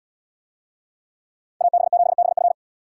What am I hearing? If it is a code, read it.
Morse code, heard as a keyed tone, about 77 wpm, 700 Hz. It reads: AB7BF